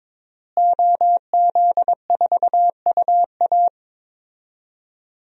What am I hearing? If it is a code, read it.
Morse code, heard as a keyed tone, about 22 wpm, 700 Hz. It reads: OZ4UA
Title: DXCC-100 Call Signs - 5 Characters - Repeated 2x - 22wpm